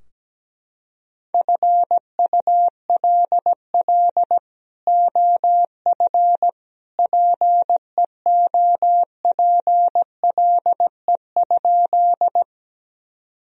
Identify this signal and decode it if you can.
Morse code, heard as a keyed tone, about 17 wpm, 700 Hz. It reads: FULL OF PEOPLE?
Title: Sentences from Top 1000 Words - Review Part 3 - 17wpm